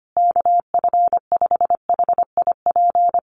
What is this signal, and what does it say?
Morse code, heard as a keyed tone, about 25 wpm, 700 Hz. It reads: KF5HIP
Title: USA Call Signs Repeated 3x 25wpm